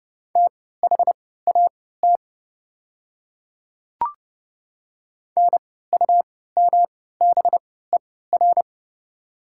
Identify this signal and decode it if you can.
Morse code, heard as a keyed tone, about 30 wpm, spaced as for 17 wpm, 690 Hz. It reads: THAT NUMBER
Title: 3x Character Spacing - Top 200 Words - 30wpm